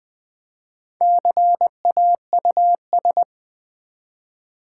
Morse code, heard as a keyed tone, about 20 wpm, 700 Hz. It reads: CAUS
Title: Build A Word - 1 Letter At A Time - 5-Letter Words - 20wpm